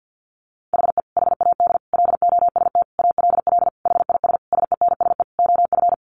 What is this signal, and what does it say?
Morse code, heard as a keyed tone, about 50 wpm, 700 Hz. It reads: HE HAD LOST ALL HIS SENSE OF
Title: Sentences from Top 1400 Words 50wpm